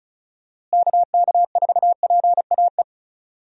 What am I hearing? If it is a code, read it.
Morse code, heard as a keyed tone, about 34 wpm, 700 Hz. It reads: KK4PAE
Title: USA Call Signs - 6 Characters - Repeated 3x - 35wpm